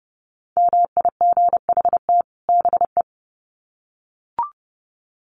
Morse code, heard as a keyed tone, about 30 wpm, 700 Hz. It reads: MIGHT BE
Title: Sets of 2 Words 30wpm